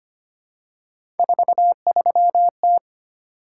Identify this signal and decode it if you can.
Morse code, heard as a keyed tone, about 25 wpm, 700 Hz. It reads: 43T
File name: Three Letter-Number Combinations - 25wpm